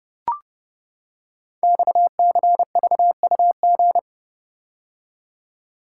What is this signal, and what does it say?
Morse code, heard as a keyed tone, about 28 wpm, 700 Hz. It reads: XCVUG
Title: Code Groups - 5 Letter:Numbers - 30wpm